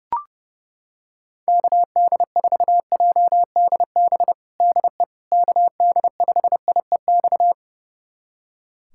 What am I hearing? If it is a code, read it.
Morse code, heard as a keyed tone, about 30 wpm, 700 Hz. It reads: KD4JDB DE KD5IEX
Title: USA Callsign DE Callsign 30wpm